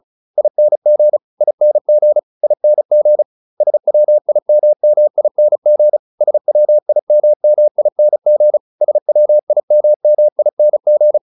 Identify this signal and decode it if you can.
Morse code, heard as a keyed tone, about 36 wpm, 600 Hz. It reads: ING ING ING SWIMMING SWIMMING SWIMMING